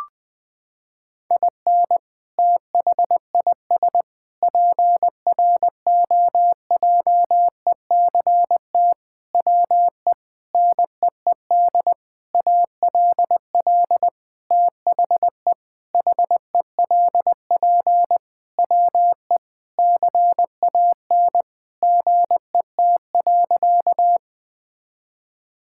Morse code, heard as a keyed tone, about 21 wpm, 700 Hz. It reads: IN THIS PROJECT WE NEED ALL THE HELP WE CAN GET.
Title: Sentences with 8-Word Phrases - 20wpm